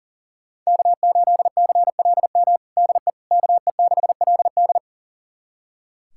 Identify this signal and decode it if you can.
Morse code, heard as a keyed tone, about 40 wpm, 700 Hz. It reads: K8CLM DE KE6LD
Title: USA Callsign DE Callsign - 5 Characters - 40wpm